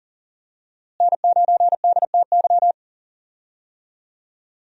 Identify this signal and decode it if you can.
Morse code, heard as a keyed tone, about 38 wpm, 700 Hz. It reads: N9DTY
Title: USA Call Signs - 5 Characters - Repeated 3x - 40wpm